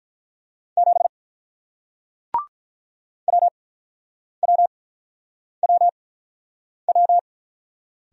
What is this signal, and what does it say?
Morse code, heard as a keyed tone, about 44 wpm, spaced as for 9 wpm, 700 Hz. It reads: 7 WWWW